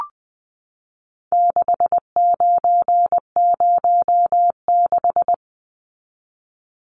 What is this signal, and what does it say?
Morse code, heard as a keyed tone, about 20 wpm, 700 Hz. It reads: 6906
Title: Numbers - 4-Digit 20wpm